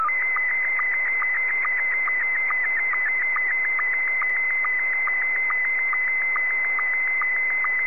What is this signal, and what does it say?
SSTV (slow-scan television), Scottie 1.